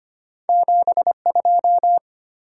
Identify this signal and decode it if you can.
Morse code, heard as a keyed tone, about 25 wpm, 700 Hz. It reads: 72
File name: Two Letter-Number Combinations - 25wpm